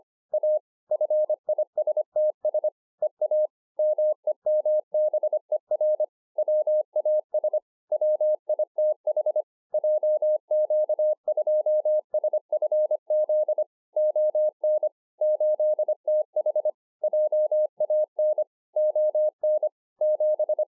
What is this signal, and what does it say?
Morse code, heard as a keyed tone, about 25 wpm, 610 Hz. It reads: A FISTS EA MEMBER WAS WITH JQ2SFZ ON 8TH JAN ON 7